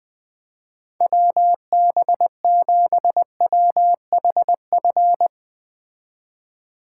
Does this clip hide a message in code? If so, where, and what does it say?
Yes: Morse code from 1.0 to 5.3 s, heard as a keyed tone, about 21 wpm, 700 Hz. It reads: WB7WHF